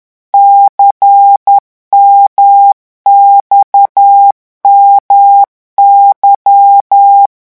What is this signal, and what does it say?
Morse code, heard as a keyed tone, about 10 wpm, 800 Hz. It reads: CMXMY
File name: morse mp3 / revision2-vitesse 10 mots